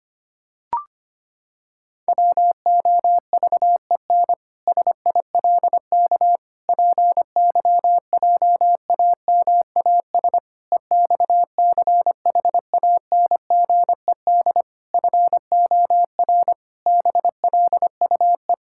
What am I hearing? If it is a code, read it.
Morse code, heard as a keyed tone, about 25 wpm, 700 Hz. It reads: WOVEN SILK PYJAMAS EXCHANGED FOR BLUE
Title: Pangram Challenge v2 - 25wpm